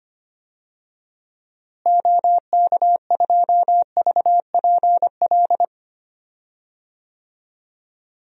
Morse code, heard as a keyed tone, about 25 wpm, 700 Hz. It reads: OK2VPL
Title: DXCC-100 Call Signs - 6 Characters - Repeated 2x - 25wpm